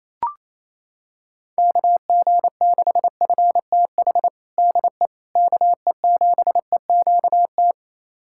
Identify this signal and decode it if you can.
Morse code, heard as a keyed tone, about 28 wpm, 700 Hz. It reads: KG6FTH DE KE7EQT
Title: USA Callsign DE Callsign - 6 Characters - 28wpm